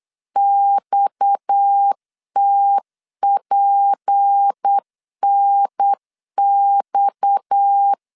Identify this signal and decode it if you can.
Morse code, heard as a keyed tone, about 8 wpm, 800 Hz. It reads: XTPNX
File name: LEZIONI / 6 PKX0 / 6 PKX 8wpm 3041